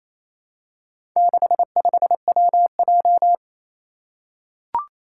Morse code, heard as a keyed tone, about 27 wpm, 700 Hz. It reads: 65WJ
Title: Four Challenging Letter-Number Combinations 28wpm